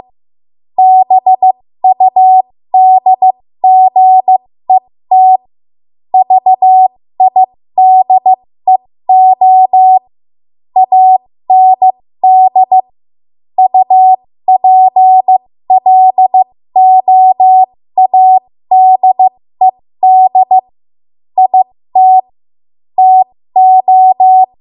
Morse code, heard as a keyed tone, about 15 wpm, spaced as for 13 wpm, 750 Hz. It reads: BUDGET VIDEO AND UPLOADED IT TO
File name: Morse / 13WPM